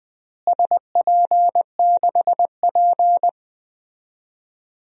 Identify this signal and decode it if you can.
Morse code, heard as a keyed tone, about 20 wpm, 700 Hz. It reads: SP6P
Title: DXCC-100 Call Signs - 4 Characters - Repeated 2x - 20wpm